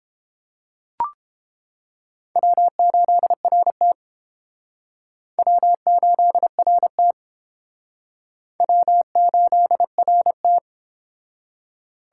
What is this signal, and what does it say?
Morse code, heard as a keyed tone, about 30 wpm, 700 Hz. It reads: W8RT W8RT W8RT